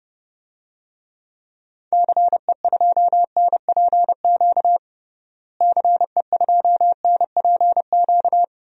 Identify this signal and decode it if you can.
Morse code, heard as a keyed tone, about 29 wpm, 700 Hz. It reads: CE2NPQ CE2NPQ
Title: DXCC-100 Call Signs - 6 Characters - Repeated 3x - 30wpm